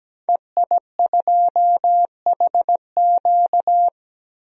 Morse code, heard as a keyed tone, about 17 wpm, 700 Hz. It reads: EI2HQ
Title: DXCC-100 Call Signs - 5 Characters - Repeated 3x - 17wpm